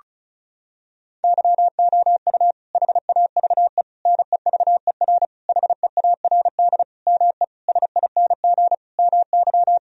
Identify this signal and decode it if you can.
Morse code, heard as a keyed tone, about 36 wpm, 700 Hz. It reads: YOU HAVE NEVER HEARD ME SING MY
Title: Sentences from Top 1000 Words - Review Part 2 - 35wpm